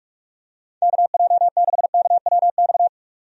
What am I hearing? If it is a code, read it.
Morse code, heard as a keyed tone, about 46 wpm, 700 Hz. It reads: KJ6KWX
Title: USA Call Signs - 6 Characters - 45wpm